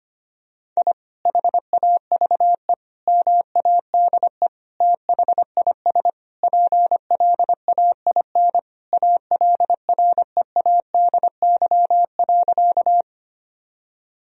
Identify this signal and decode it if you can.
Morse code, heard as a keyed tone, about 25 wpm, 700 Hz. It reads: I HAVE MADE THIS PLAIN ALREADY.
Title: Sentences from Top 1000 Words - Review Part 2 - 25wpm